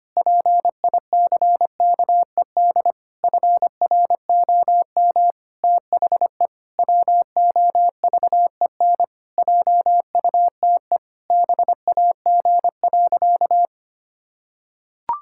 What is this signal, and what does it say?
Morse code, heard as a keyed tone, about 25 wpm, 700 Hz. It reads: PICKED FROM THE WOVEN JUTE BAG.
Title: Pangram Challenge v2 - 25wpm